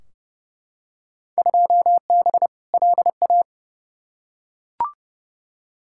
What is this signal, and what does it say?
Morse code, heard as a keyed tone, about 30 wpm, 700 Hz. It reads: 2B LA
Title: Field Day Exchanges Encore - 30wpm